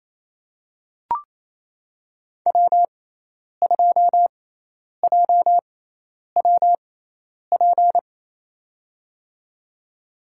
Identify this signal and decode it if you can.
Morse code, heard as a keyed tone, about 28 wpm, spaced as for 10 wpm, 700 Hz. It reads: W2JWP
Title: USA Call Signs - 5 Characters - 6x Character Spacing - 28wpm